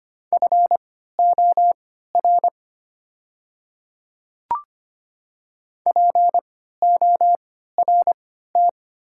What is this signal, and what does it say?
Morse code, heard as a keyed tone, about 25 wpm, spaced as for 14 wpm, 700 Hz. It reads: FOR PORT
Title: 3x Character Spacing - Top 300 Words - 25wpm